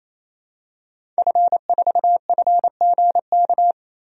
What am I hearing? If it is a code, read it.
Morse code, heard as a keyed tone, about 28 wpm, 700 Hz. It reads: F4FGK
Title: DXCC-100 Call Signs - 5 Characters - 28wpm